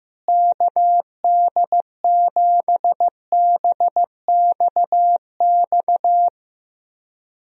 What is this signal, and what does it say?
Morse code, heard as a keyed tone, about 15 wpm, 700 Hz. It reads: KD7BXX
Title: USA Call Signs - 6 Characters - Repeated 3x - 15wpm